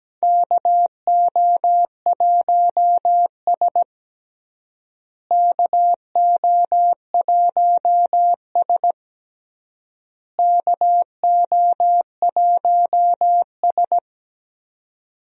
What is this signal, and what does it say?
Morse code, heard as a keyed tone, about 17 wpm, 700 Hz. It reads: KO1S KO1S KO1S
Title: USA Call Signs - 4 Characters - Repeated 3x - 17wpm